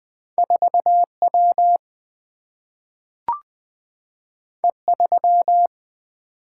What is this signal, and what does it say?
Morse code, heard as a keyed tone, about 20 wpm, 700 Hz. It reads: 4W E3
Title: Single-Letter Single-Number 20wpm